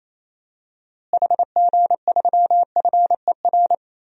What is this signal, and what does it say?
Morse code, heard as a keyed tone, about 28 wpm, 700 Hz. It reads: HG3FER
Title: DXCC-100 Call Signs - 6 Characters - Repeated 2x - 28wpm